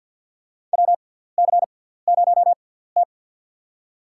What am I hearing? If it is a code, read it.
Morse code, heard as a keyed tone, about 50 wpm, spaced as for 17 wpm, 690 Hz. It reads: WC0T
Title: USA Call Signs - 4 Characters - 6x Character Spacing - 50wpm